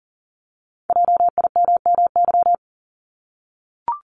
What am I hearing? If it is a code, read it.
Morse code, heard as a keyed tone, about 40 wpm, 700 Hz. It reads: JIMMY
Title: Multi-letter Phonemes 40wpm